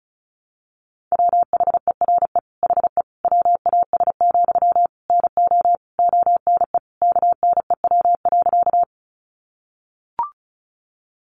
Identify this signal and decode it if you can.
Morse code, heard as a keyed tone, about 34 wpm, 700 Hz. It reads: WHERE HE WAS, NO ONE KNEW.
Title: Sentences from Top 400 Words 35wpm